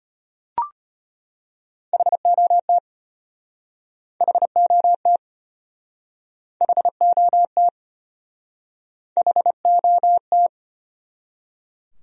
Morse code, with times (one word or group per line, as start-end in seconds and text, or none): 1.9-2.8 HOT
4.2-5.2 HOT
6.6-7.7 HOT
9.2-10.5 HOT